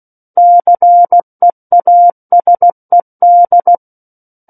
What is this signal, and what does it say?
Morse code, heard as a keyed tone, about 16 wpm, 700 Hz. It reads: CEASED
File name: Book One / 07-Chapter 7 - 16 WPM